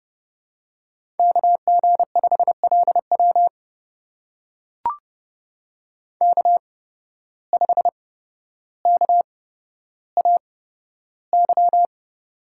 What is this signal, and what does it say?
Morse code, heard as a keyed tone, about 30 wpm, spaced as for 13 wpm, 700 Hz. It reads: KG5LW K5KAY
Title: USA Call Signs - 5 Characters - 8x Character Spacing - 30wpm